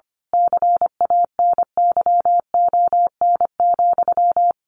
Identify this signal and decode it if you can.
Morse code, heard as a keyed tone, about 25 wpm, 700 Hz. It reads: CANYON,
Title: A Sound Of Thunder 25wpm